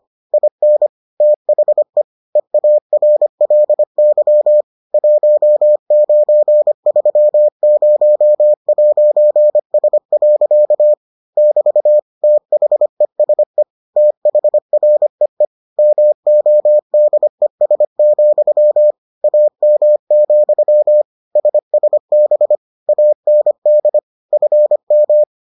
Morse code, heard as a keyed tone, about 25 wpm, 600 Hz. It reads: IN THE EARLY 1930'S. <BT> THESE THREE MODES, AM, SSB AND FM